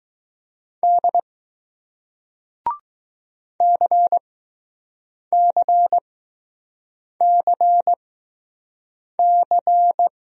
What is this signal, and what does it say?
Morse code, heard as a keyed tone, about 20 wpm, spaced as for 6 wpm, 700 Hz. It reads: D CCCC